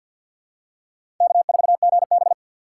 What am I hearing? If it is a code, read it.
Morse code, heard as a keyed tone, about 53 wpm, 690 Hz. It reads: K4GB
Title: USA Call Signs - 4 Characters - 6x Character Spacing - 50wpm